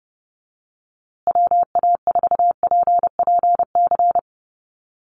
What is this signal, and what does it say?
Morse code, heard as a keyed tone, about 30 wpm, 700 Hz. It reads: WA4PPC
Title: USA Call Signs Repeated 2x 30wpm